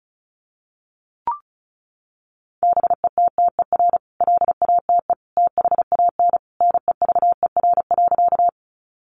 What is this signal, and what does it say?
Morse code, heard as a keyed tone, about 34 wpm, 700 Hz. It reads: BETTER LATE THAN NEVER.
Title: Sentences from Top 400 Words 35wpm